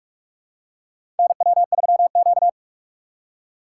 Morse code, heard as a keyed tone, about 44 wpm, 700 Hz. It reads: NW3Q
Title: USA Call Signs - 4 Characters - 2x Character Spacing - 45wpm